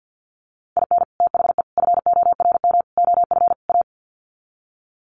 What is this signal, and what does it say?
Morse code, heard as a keyed tone, about 50 wpm, 700 Hz. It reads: IN THE FORM OF A
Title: Sets of 5 Words 50wpm